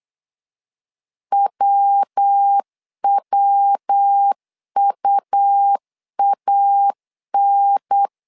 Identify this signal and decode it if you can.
Morse code, heard as a keyed tone, about 8 wpm, 800 Hz. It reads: WWUAN
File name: LEZIONI / 3 MWRO / 3 MW 8wpm 2501